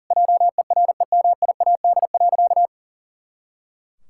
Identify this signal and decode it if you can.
Morse code, heard as a keyed tone, about 40 wpm, 700 Hz. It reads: JEREMIAD.
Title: Pangram Challenge v2 - 40wpm